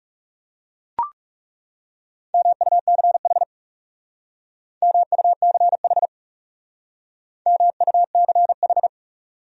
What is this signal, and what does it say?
Morse code, heard as a keyed tone, about 38 wpm, 700 Hz. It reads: MUCH MUCH MUCH